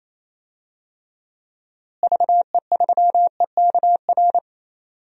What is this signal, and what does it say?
Morse code, heard as a keyed tone, about 28 wpm, 700 Hz. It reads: VE3EKR
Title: DXCC-100 Call Signs - 6 Characters - Repeated 3x - 28wpm